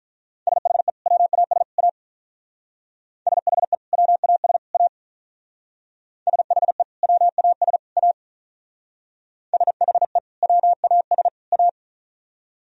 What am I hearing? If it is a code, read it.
Morse code, heard as a keyed tone, about 43 wpm, 700 Hz. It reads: SHE WAS A SHE WAS A SHE WAS A SHE WAS A
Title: Speed-Racing - Sets of 3 Words 35wpm